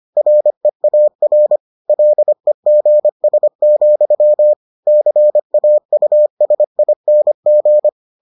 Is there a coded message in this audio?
Yes: Morse code from 0.2 to 7.9 s, heard as a keyed tone, about 25 wpm, 600 Hz. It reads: REAR LEGS, CAUSING